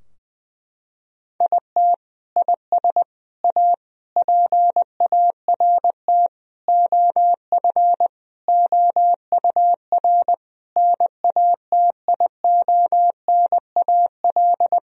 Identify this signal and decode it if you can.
Morse code, heard as a keyed tone, about 19 wpm, 700 Hz. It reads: IT IS A PART OF OUR NATIONAL
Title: Sentences from Top 1000 Words - Review Part 4 - 20wpm